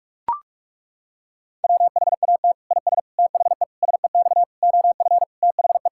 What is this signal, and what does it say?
Morse code, heard as a keyed tone, about 41 wpm, 700 Hz. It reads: WHAT IS THE SEX OF THE
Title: Sentences from Top 1600 Words 45wpm